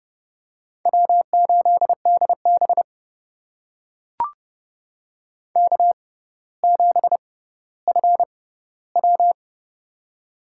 Morse code, heard as a keyed tone, about 29 wpm, spaced as for 15 wpm, 700 Hz. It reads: W8DB K7FW
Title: USA Call Signs - 4 Characters - 6x Character Spacing - 30wpm